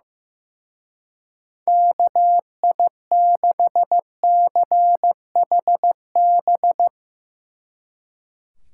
Morse code, heard as a keyed tone, about 15 wpm, 700 Hz. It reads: KI6CHB
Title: Speed Racing - USA 2x3 Call Signs 15wpm